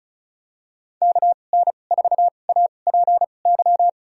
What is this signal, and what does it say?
Morse code, heard as a keyed tone, about 34 wpm, spaced as for 25 wpm, 700 Hz. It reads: KN4APY
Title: USA Call Signs - 6 Characters - 2x Character Spacing - 35wpm